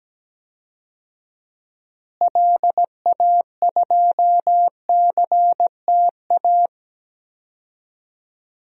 Morse code, heard as a keyed tone, about 17 wpm, 700 Hz. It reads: LA2CTA
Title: DXCC-100 Call Signs - 6 Characters - Repeated 2x - 17wpm